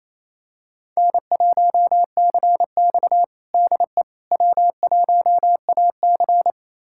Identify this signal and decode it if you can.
Morse code, heard as a keyed tone, about 28 wpm, 700 Hz. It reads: N1CX DE W1AC
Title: USA Callsign DE Callsign - 4 Characters - 28wpm